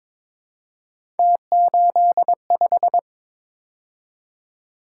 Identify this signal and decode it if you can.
Morse code, heard as a keyed tone, about 22 wpm, 700 Hz. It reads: T85
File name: practice / 22wpm / Three Letter-Number Combinations - 22wpm